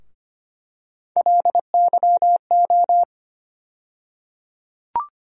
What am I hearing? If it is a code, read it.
Morse code, heard as a keyed tone, about 25 wpm, 700 Hz. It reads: LYO
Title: USA Call Sign Postfixes - 3 Letters 25wpm